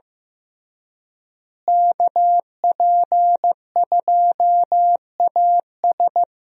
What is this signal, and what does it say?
Morse code, heard as a keyed tone, about 15 wpm, 700 Hz. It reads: KP2AS
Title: USA Call Signs - 5 Characters - Repeated 3x - 15wpm